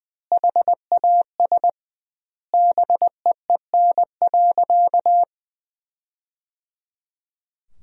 Morse code, heard as a keyed tone, about 20 wpm, 700 Hz. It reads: HAS BEEN.
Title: 2x Word-Spacing - Sentences from Top 300 Words - 20wpm